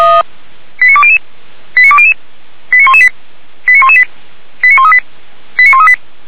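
Five-tone selective call (ZVEI), heard as quick runs of five tones, tones of about 70 ms. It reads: 89200 89200 89118 89118 89138 89138
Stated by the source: ZVEI five-tone selective call 89227 received with an SDR by Feuerwehr Orlamünde.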